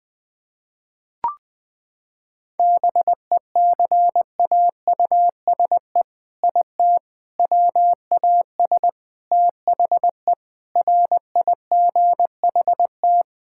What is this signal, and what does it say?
Morse code, heard as a keyed tone, about 20 wpm, 700 Hz. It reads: BECAUSE IT WAS THE RIGHT
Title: Sets of 8 Words - 20wpm